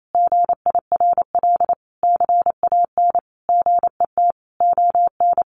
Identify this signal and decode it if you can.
Morse code, heard as a keyed tone, about 28 wpm, 700 Hz. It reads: GIRL CAN GET ON
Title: Sentences from Top 1400 Words 28wpm